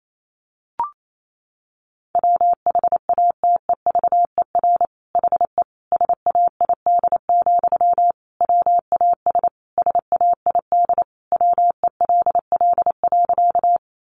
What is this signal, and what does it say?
Morse code, heard as a keyed tone, about 28 wpm, 700 Hz. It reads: WHATEVER HE SAID, WAS SAID WELL.
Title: Sentences from Top 1100 Words 28wpm